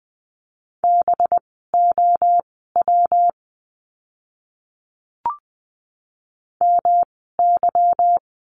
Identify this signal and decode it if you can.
Morse code, heard as a keyed tone, about 20 wpm, spaced as for 15 wpm, 700 Hz. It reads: BOW MY
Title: 185 Lesson 30f - Letter b - Words 20wpm